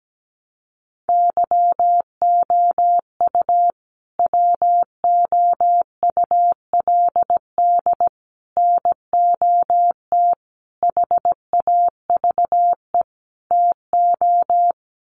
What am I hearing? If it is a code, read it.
Morse code, heard as a keyed tone, about 17 wpm, 700 Hz. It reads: YOU WOULD NOT HAVE TO
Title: Sets of 5 Words Encore 17wpm